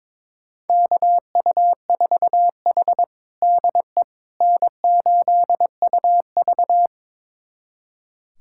Morse code, heard as a keyed tone, about 22 wpm, 700 Hz. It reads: KU4H DE N8UV
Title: USA Callsign DE Callsign - 4 Characters - 22wpm